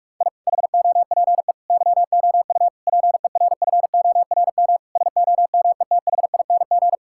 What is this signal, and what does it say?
Morse code, heard as a keyed tone, about 46 wpm, 700 Hz. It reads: I HOPE YOU PERFORM SOMETHING
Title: Sentences from Top 1000 Words - Review Part 6 - 45wpm